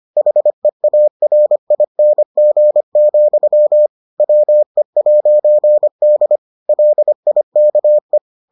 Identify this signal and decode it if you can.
Morse code, heard as a keyed tone, about 25 wpm, 600 Hz. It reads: HEARING, WE'D LIKE